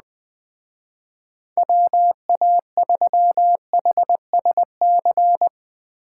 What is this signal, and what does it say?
Morse code, heard as a keyed tone, about 20 wpm, 700 Hz. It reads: WA3HSC